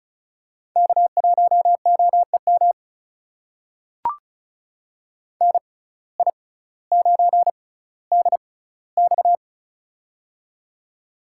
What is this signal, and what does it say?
Morse code, heard as a keyed tone, about 36 wpm, spaced as for 18 wpm, 700 Hz. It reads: K1OEM NI9DX
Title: USA Call Signs - 5 Characters - 6x Character Spacing - 35wpm